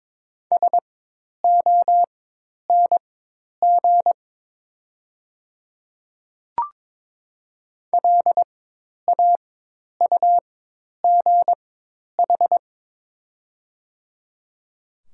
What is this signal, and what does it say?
Morse code, heard as a keyed tone, about 22 wpm, spaced as for 10 wpm, 700 Hz. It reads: SONG LAUGH
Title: 4x Character Spacing - Top 500 Words - 22wpm